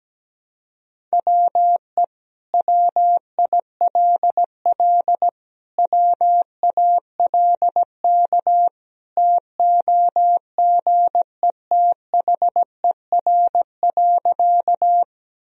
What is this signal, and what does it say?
Morse code, heard as a keyed tone, about 17 wpm, 700 Hz. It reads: WE WILL WALK TOGETHER.
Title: Sentences from Top 1000 Words - Review Part 1 - 17wpm